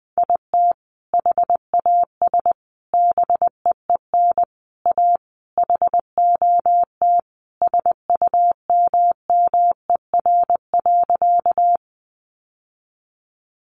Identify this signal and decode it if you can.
Morse code, heard as a keyed tone, about 20 wpm, 700 Hz. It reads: IT HAS BEEN A HOT SUMMER.
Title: Sentences from Top 1100 Words 20wpm